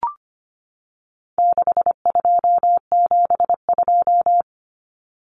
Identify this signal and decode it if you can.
Morse code, heard as a keyed tone, about 25 wpm, 700 Hz. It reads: 6272
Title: Numbers - 4-Digit 25wpm